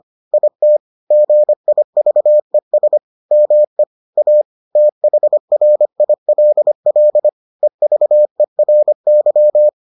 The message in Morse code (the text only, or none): IT GIVES ME A THRILL EVERY